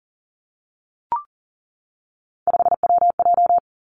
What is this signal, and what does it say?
Morse code, heard as a keyed tone, about 40 wpm, 700 Hz. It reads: 5WJ
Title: Three Challenging Letter-Number Combinations 40wpm